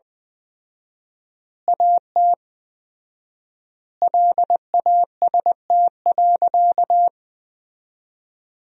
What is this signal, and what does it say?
Morse code, heard as a keyed tone, about 20 wpm, 700 Hz. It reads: AT LAST.
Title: 4x Word-Spacing - Sentences from Top 300 Words - 20wpm